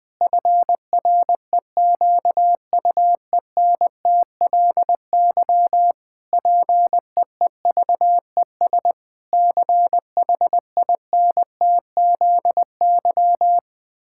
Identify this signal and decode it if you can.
Morse code, heard as a keyed tone, about 21 wpm, 700 Hz. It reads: FREQUENTLY PEEVES CHINTZY
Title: Pangram Challenge v2 - 20wpm